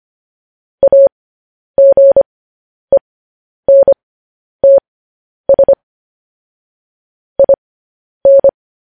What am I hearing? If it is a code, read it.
Morse code, heard as a keyed tone, about 25 wpm, spaced as for 10 wpm, 550 Hz. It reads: AGENTS IN
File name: podcast / News headlines-1x-25x10